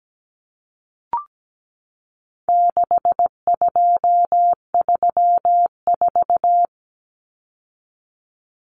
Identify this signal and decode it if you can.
Morse code, heard as a keyed tone, about 17 wpm, 700 Hz. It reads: 6234
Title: Common Cut Numbers - 4-Digit 17wpm